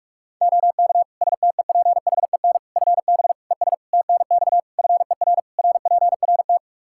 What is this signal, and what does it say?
Morse code, heard as a keyed tone, about 46 wpm, 700 Hz. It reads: OK STEPHEN FB ES TNX FER RPRT
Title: Practice QSO - Exchange 2 - Info - 45wpm